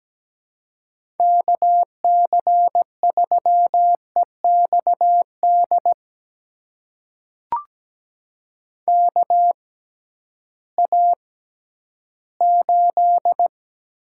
Morse code, heard as a keyed tone, about 17 wpm, spaced as for 11 wpm, 700 Hz. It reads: KC3EXD KA8
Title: USA Call Signs - 6 Characters - 6x Character Spacing - 17wpm